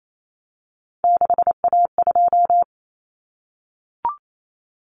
Morse code, heard as a keyed tone, about 28 wpm, 700 Hz. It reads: 6A2
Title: Common Cut Numbers - 3-Digit 28wpm